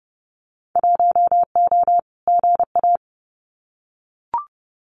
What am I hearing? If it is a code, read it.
Morse code, heard as a keyed tone, about 30 wpm, 700 Hz. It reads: 1O GA
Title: Winter Field Day Exchanges 30wpm